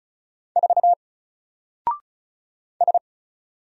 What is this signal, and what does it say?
Morse code, heard as a keyed tone, about 36 wpm, 700 Hz. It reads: 4 S
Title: Single Character (Letter:Number:Punctuation:Prosign) Alt 35wpm